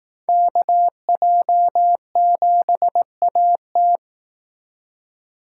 Morse code, heard as a keyed tone, about 18 wpm, 700 Hz. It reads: KJ7AT
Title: Speed Racing - USA 2x2 Call Signs 15wpm